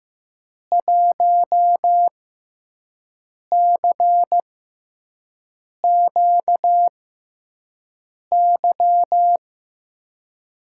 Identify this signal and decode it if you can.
Morse code, heard as a keyed tone, about 15 wpm, spaced as for 5 wpm, 700 Hz. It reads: 1CQY
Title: USA Call Signs - 6 Characters - 6x Character Spacing - 15wpm